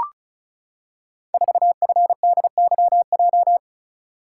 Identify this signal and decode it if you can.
Morse code, heard as a keyed tone, about 36 wpm, 700 Hz. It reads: 4FDYJ
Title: Code Groups - 5 Letter:Numbers - 35wpm